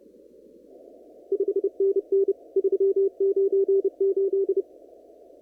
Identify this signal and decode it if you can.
Morse code, heard as a keyed tone, about 32 wpm, 390 Hz. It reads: 5NN 398
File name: CW / kt5j wpx cw-op-k5pi